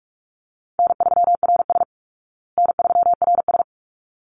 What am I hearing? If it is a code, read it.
Morse code, heard as a keyed tone, about 44 wpm, 700 Hz. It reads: N3RS N3RS
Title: Easy Call Signs Repeated 2x - 45wpm